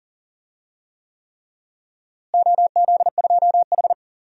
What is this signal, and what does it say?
Morse code, heard as a keyed tone, about 39 wpm, 700 Hz. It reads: OZ2H
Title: DXCC-100 Call Signs - 4 Characters - Repeated 3x - 40wpm